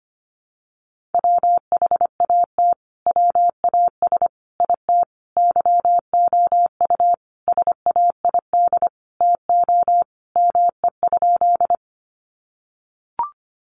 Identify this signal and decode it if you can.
Morse code, heard as a keyed tone, about 25 wpm, 700 Hz. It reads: WHAT WAS IT YOU SAID TO ME?
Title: Sentences from Top 200 Words 25wpm